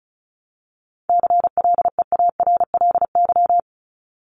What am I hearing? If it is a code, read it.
Morse code, heard as a keyed tone, about 34 wpm, 700 Hz. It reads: CLEARLY
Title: Top 1001-1250 Words 35wpm